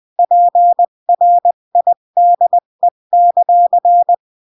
Morse code, heard as a keyed tone, about 20 wpm, 700 Hz. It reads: PRIDE;
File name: CW / jodorowsky / 82 commandments 20-15